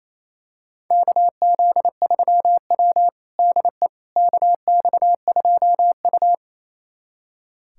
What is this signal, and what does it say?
Morse code, heard as a keyed tone, about 27 wpm, 700 Hz. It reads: KZ3W DE KX2U
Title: USA Callsign DE Callsign - 4 Characters - 28wpm